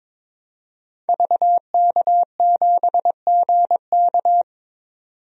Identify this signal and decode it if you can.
Morse code, heard as a keyed tone, about 22 wpm, 700 Hz. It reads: VK7GK
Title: DXCC-100 Call Signs - 5 Characters - 22wpm